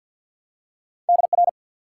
Morse code, heard as a keyed tone, about 54 wpm, 700 Hz. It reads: DR
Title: Top 90 QSO Elements 50wpm